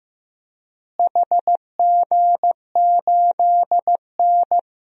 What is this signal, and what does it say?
Morse code, heard as a keyed tone, about 15 wpm, 700 Hz. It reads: HG8N
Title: DXCC-100 Call Signs - 4 Characters - Repeated 2x - 15wpm